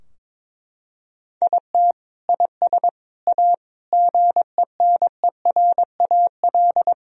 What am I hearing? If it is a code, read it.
Morse code, heard as a keyed tone, about 22 wpm, 700 Hz. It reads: IT IS A GENERAL
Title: Sentences from Top 1000 Words - Review Part 3 - 22wpm